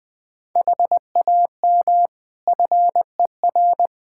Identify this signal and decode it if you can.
Morse code, heard as a keyed tone, about 20 wpm, 700 Hz. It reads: HAM FER
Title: Practice QSO - Exchange 3 - More Info - 20wpm